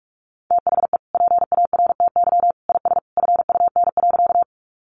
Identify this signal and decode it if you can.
Morse code, heard as a keyed tone, about 46 wpm, 700 Hz. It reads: THE PARTY IS FUN.
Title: Easy Phrases 45wpm